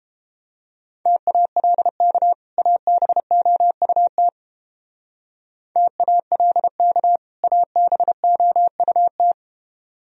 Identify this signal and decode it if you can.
Morse code, heard as a keyed tone, about 31 wpm, 700 Hz. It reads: TALK ABOUT TALK ABOUT